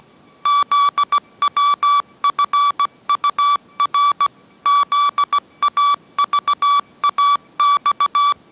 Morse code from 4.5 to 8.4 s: ZAVAX